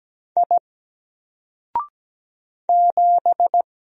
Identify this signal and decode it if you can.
Morse code, heard as a keyed tone, about 17 wpm, 700 Hz. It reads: I 7
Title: Single Letter-Number Alt 17wpm